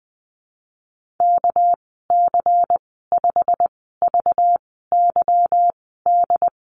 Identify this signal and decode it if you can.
Morse code, heard as a keyed tone, about 20 wpm, spaced as for 15 wpm, 700 Hz. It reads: KC5VYD
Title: 213 Lesson 34g - Number 8 - Call Signs 20wpm